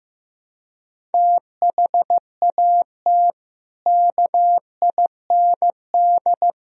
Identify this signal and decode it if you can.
Morse code, heard as a keyed tone, about 15 wpm, 700 Hz. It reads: THAT KIND
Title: Sentences from Top 1000 Words - Review Part 4 - 15wpm